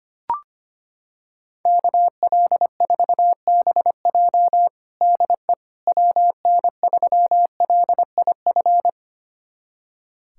Morse code, heard as a keyed tone, about 25 wpm, 700 Hz. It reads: KL4BJ DE WN3LIF
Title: USA Callsign DE Callsign 25wpm